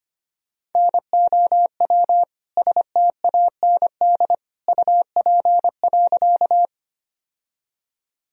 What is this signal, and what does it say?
Morse code, heard as a keyed tone, about 25 wpm, 700 Hz. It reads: NOW STAND UP.
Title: Sentences from Top 1000 Words - Review Part 1 - 25wpm